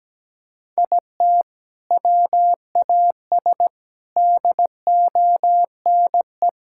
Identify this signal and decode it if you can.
Morse code, heard as a keyed tone, about 17 wpm, 700 Hz. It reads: IT WAS DONE
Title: Sentences from Top 1600 Words 17wpm